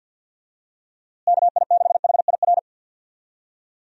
Morse code, heard as a keyed tone, about 46 wpm, 700 Hz. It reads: KI6HIR
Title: USA Call Signs - 6 Characters - Repeated 3x - 50wpm